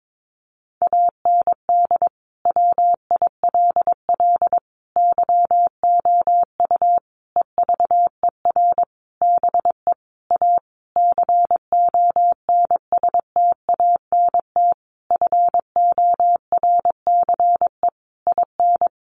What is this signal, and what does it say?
Morse code, heard as a keyed tone, about 22 wpm, 700 Hz. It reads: AND WILL YOU EVER BE A CONSTANT FORCE IN
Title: Sentences from Top 1500 Words 22wpm